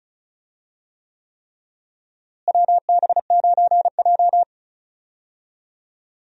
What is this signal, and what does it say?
Morse code, heard as a keyed tone, about 36 wpm, 700 Hz. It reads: WB9J
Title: USA Call Signs - 4 Characters - Repeated 3x - 35wpm